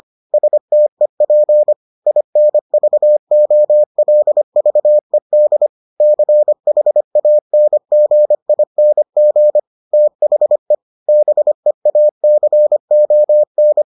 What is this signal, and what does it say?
Morse code, heard as a keyed tone, about 25 wpm, 600 Hz. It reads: STEP INVOLVED CHANGING THE BEACON